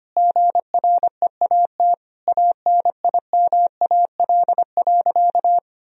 Morse code, heard as a keyed tone, about 25 wpm, 700 Hz. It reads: GREAT ANIMAL.
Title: Sentences from Top 1000 Words - Review Part 1 - 25wpm